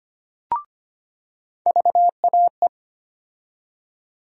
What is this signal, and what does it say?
Morse code, heard as a keyed tone, about 25 wpm, 700 Hz. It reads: VAE
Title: USA Call Sign Postfixes - 3 Letters 25wpm